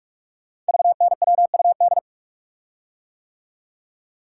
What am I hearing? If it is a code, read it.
Morse code, heard as a keyed tone, about 44 wpm, 690 Hz. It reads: VNWUD
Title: Code Groups - 5 Letters - 45wpm